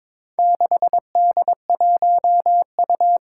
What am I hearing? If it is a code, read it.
Morse code, heard as a keyed tone, about 22 wpm, 700 Hz. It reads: 6D1U
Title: Four Challenging Letter-Number Combinations 22wpm